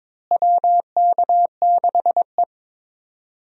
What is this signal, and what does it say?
Morse code, heard as a keyed tone, about 22 wpm, 700 Hz. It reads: WK6E
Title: USA Call Signs - 4 Characters - Repeated 3x - 22wpm